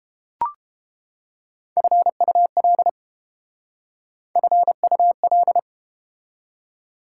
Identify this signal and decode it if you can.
Morse code, heard as a keyed tone, about 30 wpm, 700 Hz. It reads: FUL FUL